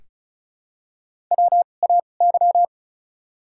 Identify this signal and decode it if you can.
Morse code, heard as a keyed tone, about 34 wpm, spaced as for 25 wpm, 700 Hz. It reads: WAY
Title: 2x Character Spacing - Top 500 Words - 35wpm